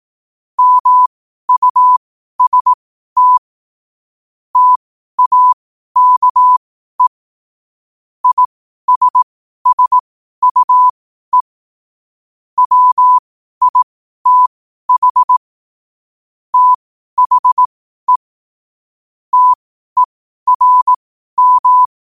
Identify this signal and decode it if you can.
Morse code, heard as a keyed tone, about 18 wpm, spaced as for 12 wpm, 1000 Hz. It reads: MUST TAKE ISSUE WITH THE TERM
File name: qotdfiles / 12wpm